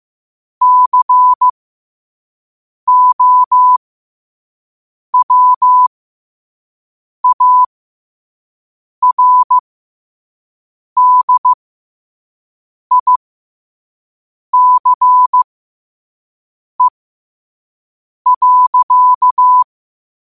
Morse code, heard as a keyed tone, about 15 wpm, spaced as for 5 wpm, 1000 Hz. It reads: COWARDICE.
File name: qotdfiles / 5wpm